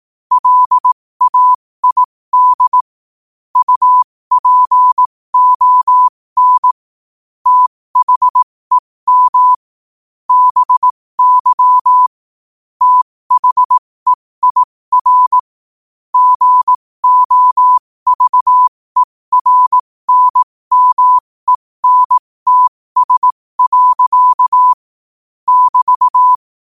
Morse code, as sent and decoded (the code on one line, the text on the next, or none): .-.. .- .. -.. / ..- .--. --- -. / - .... -- / -... -.-- / - .... .. .-. / --. --- ...- . .-. -. -- . -. - ... .-.-.- / -...-
LAID UPON THEM BY THEIR GOVERNMENTS. <BT>